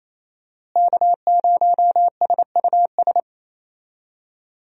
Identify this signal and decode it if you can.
Morse code, heard as a keyed tone, about 28 wpm, 700 Hz. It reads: K0SUS
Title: USA Call Signs - 5 Characters - Repeated 2x - 28wpm